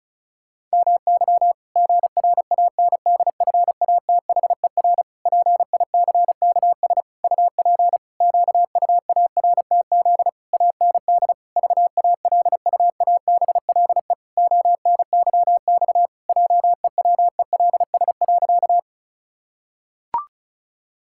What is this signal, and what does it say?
Morse code, heard as a keyed tone, about 34 wpm, 700 Hz. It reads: MY GRANDFATHER PICKS UP QUARTZ AND VALUABLE ONYX JEWELS.
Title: Pangram Challenge v2 - 35wpm